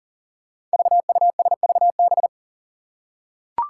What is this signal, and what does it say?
Morse code, heard as a keyed tone, about 40 wpm, 700 Hz. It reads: VUSVB
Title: Five Challenging Letter-Number Combinations 40wpm